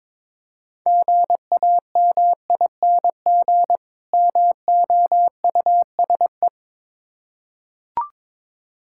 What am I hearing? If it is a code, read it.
Morse code, heard as a keyed tone, about 22 wpm, 700 Hz. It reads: GAMING MOUSE
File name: Difficult Words with Hints - 22wpm